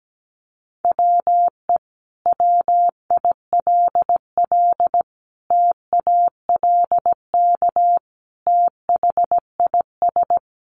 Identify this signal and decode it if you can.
Morse code, heard as a keyed tone, about 17 wpm, 700 Hz. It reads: WE WILL TALK THIS
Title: Sentences from Top 700 Words 17wpm